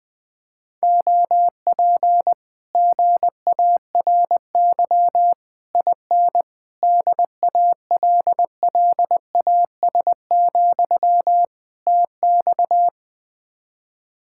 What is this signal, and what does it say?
Morse code, heard as a keyed tone, about 20 wpm, 700 Hz. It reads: OP GARY IN DALLAS, TX
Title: Speed-Racing Common QSO Phrases Encore - 17wpm